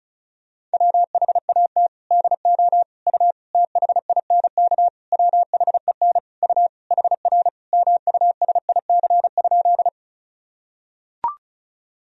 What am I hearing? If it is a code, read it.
Morse code, heard as a keyed tone, about 36 wpm, 700 Hz. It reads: WHAT DO U THINK WHEN U HR MUSIC?
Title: Sentences from Top 300 Words and Abbreviations 35wpm